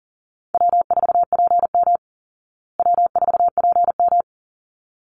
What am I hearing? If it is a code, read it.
Morse code, heard as a keyed tone, about 40 wpm, 700 Hz. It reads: W4PM W4PM
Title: Easy Call Signs Repeated 2x - 40wpm